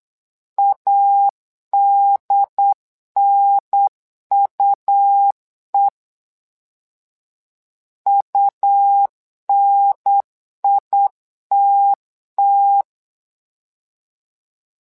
Morse code, heard as a keyed tone, about 8 wpm, 800 Hz. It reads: ADNUE UNITT
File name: LEZIONI / 2 UHDN / 2 UHDN 8wpm2232